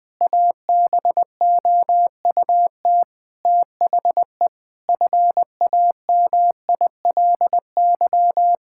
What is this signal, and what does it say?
Morse code, heard as a keyed tone, about 19 wpm, 700 Hz. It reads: ABOUT THE FAMILY
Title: Sentences from Top 1000 Words - Review Part 6 - 20wpm